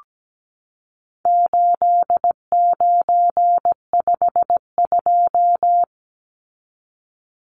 Morse code, heard as a keyed tone, about 17 wpm, 700 Hz. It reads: 8952
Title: Numbers - 4-Digit 17wpm